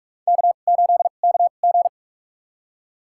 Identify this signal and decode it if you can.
Morse code, heard as a keyed tone, about 44 wpm, spaced as for 32 wpm, 700 Hz. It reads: K8KG
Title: USA Call Signs - 4 Characters - 2x Character Spacing - 45wpm